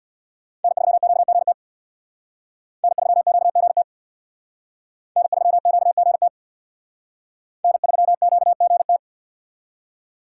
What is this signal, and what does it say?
Morse code, heard as a keyed tone, about 64 wpm, 700 Hz. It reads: N3QGT N3QGT N3QGT N3QGT